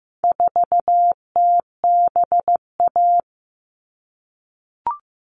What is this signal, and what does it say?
Morse code, heard as a keyed tone, about 15 wpm, 700 Hz. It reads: 4TBA
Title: All Cut Numbers - 4-Digit 15wpm